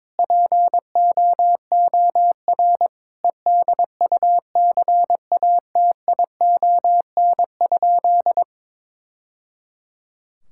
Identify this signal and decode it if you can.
Morse code, heard as a keyed tone, about 22 wpm, 700 Hz. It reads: POOR EDUCATION?
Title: Sentences from Top 1000 Words - Review Part 6 - 22wpm